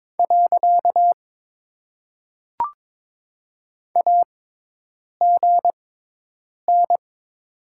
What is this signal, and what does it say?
Morse code, heard as a keyed tone, about 22 wpm, spaced as for 8 wpm, 700 Hz. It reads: . AGN
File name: Top 90 QSO Elements - 6x Character Spacing - 22wpm